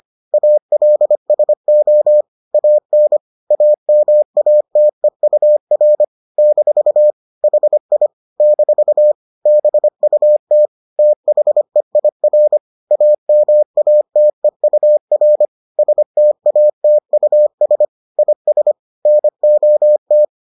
Morse code, heard as a keyed tone, about 25 wpm, 600 Hz. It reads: ALSO AN AMATEUR - HI - BUT THEIR AMATEUR STATUS IS NOT